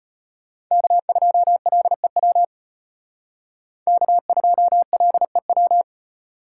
Morse code, heard as a keyed tone, about 32 wpm, 700 Hz. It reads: K2LEW K2LEW